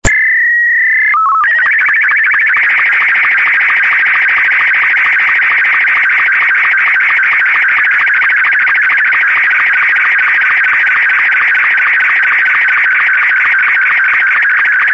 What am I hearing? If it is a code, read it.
SSTV (slow-scan television), Martin 2.
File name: sstv / sstv2 / martin m2